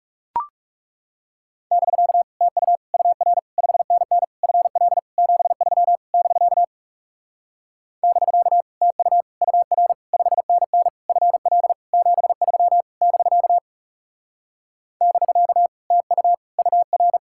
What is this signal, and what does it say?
Morse code, heard as a keyed tone, about 40 wpm, 700 Hz. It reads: <BK> TU UR 5NN FL 73 <BK> <BK> TU UR 5NN FL 73 <BK> <BK> TU UR